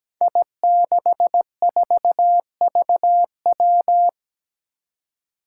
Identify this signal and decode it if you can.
Morse code, heard as a keyed tone, about 17 wpm, 700 Hz. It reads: I64VW
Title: Five Challenging Letter-Number Combinations 17wpm